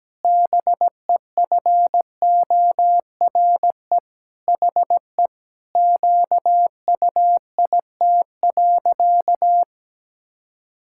Morse code, heard as a keyed tone, about 17 wpm, 700 Hz. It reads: BEFORE HE QUIT.
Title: Pangram Challenge v2 - 17wpm